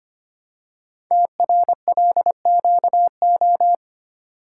Morse code, heard as a keyed tone, about 25 wpm, 700 Hz. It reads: TRLQO
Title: Code Groups - 5 Letter:Numbers - 25wpm